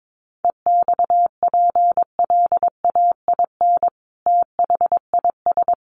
Morse code, heard as a keyed tone, about 22 wpm, 700 Hz. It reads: EXPLAIN THIS
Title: Sentences from Top 1200 Words 22wpm